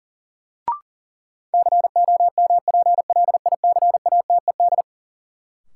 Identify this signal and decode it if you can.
Morse code, heard as a keyed tone, about 40 wpm, 700 Hz. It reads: COMPLICATED
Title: Sentences from Top 1200 Words - New Word Drill Alt - 40wpm